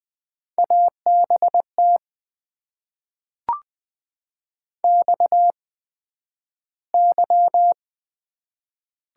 Morse code, heard as a keyed tone, about 20 wpm, spaced as for 11 wpm, 700 Hz. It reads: ABT XY
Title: Top 90 QSO Elements - 8x Character Spacing - 20wpm